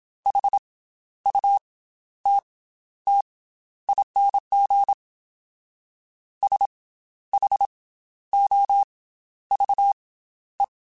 Morse code, heard as a keyed tone, about 27 wpm, spaced as for 12 wpm, 800 Hz. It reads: HUTTING SHOVE
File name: Special INGs / Special INGs-mp3s / Slow ing @25